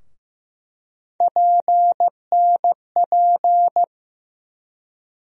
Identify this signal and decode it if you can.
Morse code, heard as a keyed tone, about 15 wpm, 700 Hz. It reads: PNP
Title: USA Call Sign Postfixes - 3 Letters 15wpm